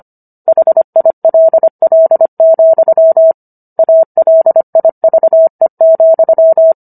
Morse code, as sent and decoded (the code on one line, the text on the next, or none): .... .. .-.. .-.. --..-- / .- .-.. .. ...- . --..--
HILL, ALIVE,